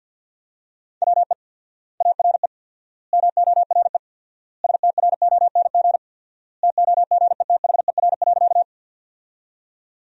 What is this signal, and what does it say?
Morse code, heard as a keyed tone, about 48 wpm, 700 Hz. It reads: WE ARE MORE STRONG TOGETHER.
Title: 4x Word-Spacing - Sentences from Top 500 Words - 50wpm